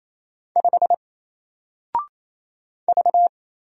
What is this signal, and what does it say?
Morse code, heard as a keyed tone, about 28 wpm, 700 Hz. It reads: 5 V
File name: Single-Letter-Number-28wpm